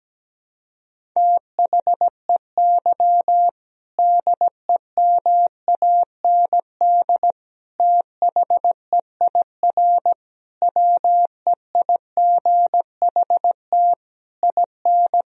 Morse code, heard as a keyed tone, about 17 wpm, 700 Hz. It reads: THEY DEMAND THEIR WEIGHT IN
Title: Sentences from Top 1000 Words - Review Part 3 - 17wpm